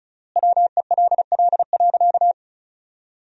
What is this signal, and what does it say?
Morse code, heard as a keyed tone, about 34 wpm, 700 Hz. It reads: WELL.
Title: 3x Word-Spacing - Sentences from Top 500 Words - 35wpm